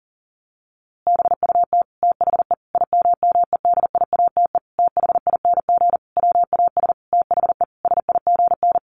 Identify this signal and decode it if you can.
Morse code, heard as a keyed tone, about 40 wpm, 700 Hz. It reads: BUT THE IMMEDIATE THING WAS THE SIGN
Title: A Sound Of Thunder 40wpm